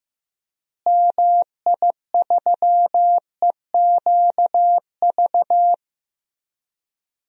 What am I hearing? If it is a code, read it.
Morse code, heard as a keyed tone, about 14 wpm, 700 Hz. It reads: MI3EQV
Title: DXCC-100 Call Signs - 6 Characters - Repeated 3x - 15wpm